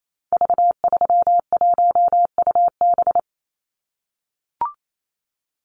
Morse code, heard as a keyed tone, about 28 wpm, 700 Hz. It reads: V31UB
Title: Call Signs - 28wpm